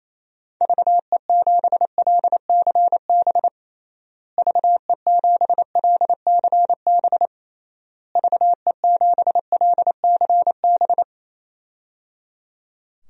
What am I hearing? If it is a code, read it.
Morse code, heard as a keyed tone, about 28 wpm, 700 Hz. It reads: VE7LCB VE7LCB VE7LCB
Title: DXCC-100 Call Signs - 6 Characters - Repeated 3x - 28wpm